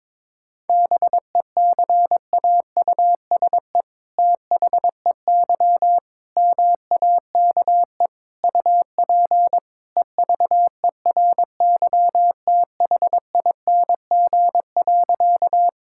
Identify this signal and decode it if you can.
Morse code, heard as a keyed tone, about 22 wpm, 700 Hz. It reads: BECAUSE THEY MAKE UP EVERYTHING.
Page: morse-code-ninja-practice-22wpm